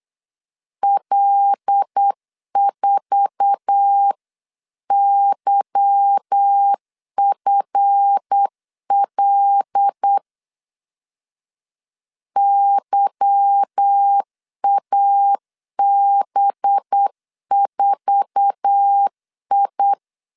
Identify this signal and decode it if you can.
Morse code, heard as a keyed tone, about 8 wpm, 800 Hz. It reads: L4YFL YAB4I
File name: LEZIONI / 8 YJ45 / 8 YJ4 8wpm 3246